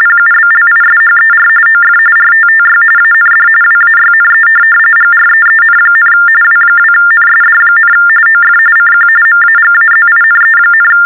TTY/TDD text telephone tones, about 45.45 baud. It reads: PUBLISHER OF RTTY AND WAYNE GREEN, W2NSD, OF CQ MAGAZINE,
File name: teletype-RTTY-45